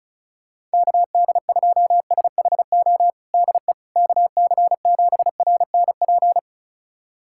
Morse code, heard as a keyed tone, about 36 wpm, 700 Hz. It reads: KD2SHO DE KC7RNP
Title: USA Callsign DE Callsign - 6 Characters - 35wpm